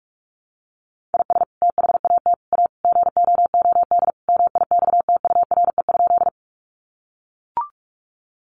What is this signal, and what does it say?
Morse code, heard as a keyed tone, about 44 wpm, 700 Hz. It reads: IS THAT A GOOD MIXTURE?
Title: Sentences from Top 1500 Words 45wpm